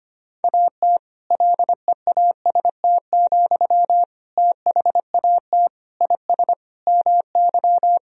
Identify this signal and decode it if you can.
Morse code, heard as a keyed tone, about 25 wpm, 700 Hz. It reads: AT LEAST, THAT IS MY
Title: Sentences from Top 1000 Words - Review Part 3 - 25wpm